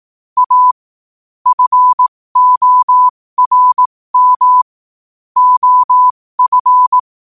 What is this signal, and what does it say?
Morse code, heard as a keyed tone, about 18 wpm, spaced as for 15 wpm, 1000 Hz. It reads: A FORM OF
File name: qotdfiles / 15wpm